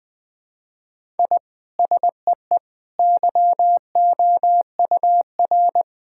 Morse code, heard as a keyed tone, about 19 wpm, 700 Hz. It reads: I SEE YOUR
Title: Sentences from Top 1000 Words - Review Part 6 - 20wpm